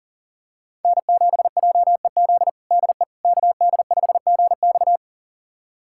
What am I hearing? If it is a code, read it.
Morse code, heard as a keyed tone, about 37 wpm, 700 Hz. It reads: N7JEZ DE KD5GX
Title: USA Callsign DE Callsign - 5 Characters - 40wpm